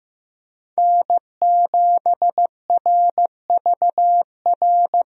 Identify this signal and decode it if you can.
Morse code, heard as a keyed tone, about 15 wpm, 700 Hz. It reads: N7RVR
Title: USA Call Signs - 5 Characters - Repeated 2x - 15wpm